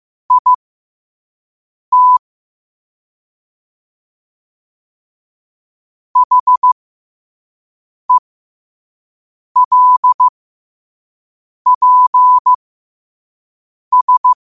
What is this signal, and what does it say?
Morse code, heard as a keyed tone, about 15 wpm, spaced as for 5 wpm, 1000 Hz. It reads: IT HELPS
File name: 5wpm